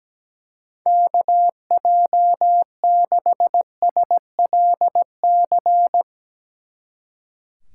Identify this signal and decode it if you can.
Morse code, heard as a keyed tone, about 18 wpm, 700 Hz. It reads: KJ6SLC
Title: Speed Racing - USA 2x3 Call Signs 17wpm